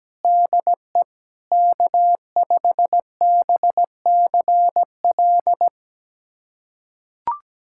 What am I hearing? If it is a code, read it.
Morse code, heard as a keyed tone, about 17 wpm, 700 Hz. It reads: DE K5BCL
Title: USA Callsign DE Callsign - 5 Characters - 17wpm